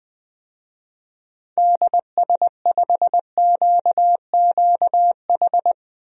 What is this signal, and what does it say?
Morse code, heard as a keyed tone, about 20 wpm, 700 Hz. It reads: DS5QQH
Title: DXCC-100 Call Signs - 6 Characters - Repeated 3x - 20wpm